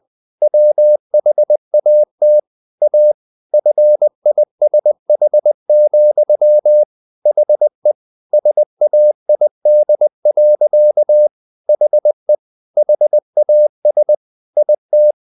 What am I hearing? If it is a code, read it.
Morse code, heard as a keyed tone, about 20 wpm, 600 Hz. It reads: WHAT A FISH, HE SAID. HE HAS IT
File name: the old man and the sea / the old man and the sea part 1 20wpm